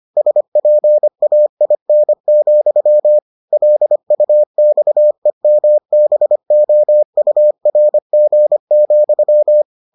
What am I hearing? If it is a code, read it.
Morse code, heard as a keyed tone, about 25 wpm, 600 Hz. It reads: SPAIN, LUXEMBOURG,